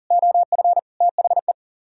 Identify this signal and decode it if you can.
Morse code, heard as a keyed tone, about 51 wpm, 700 Hz. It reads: OF THE